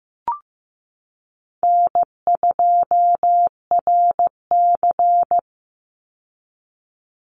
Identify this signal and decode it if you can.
Morse code, heard as a keyed tone, about 15 wpm, 700 Hz. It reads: N2RC
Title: Easy Call Signs - 15wpm